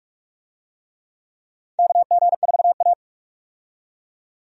Morse code, heard as a keyed tone, about 45 wpm, 700 Hz. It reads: KG4A
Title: USA Call Signs - 4 Characters - 45wpm